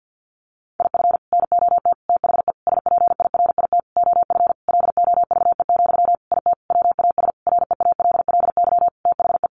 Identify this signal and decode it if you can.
Morse code, heard as a keyed tone, about 49 wpm, 700 Hz. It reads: IF NOT THE SPIRIT OF LOVE, IT WAS REALLY THE
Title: Sentences from Top 1400 Words 50wpm